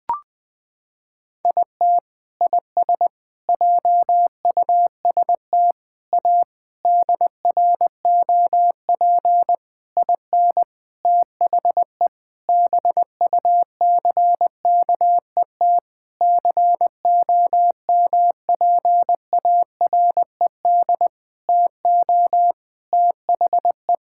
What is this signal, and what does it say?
Morse code, heard as a keyed tone, about 20 wpm, 700 Hz. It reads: IT IS JUST A DROP IN THE BUCKET COMPARED TO THE
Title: Sentences with 8-Word Phrases - 20wpm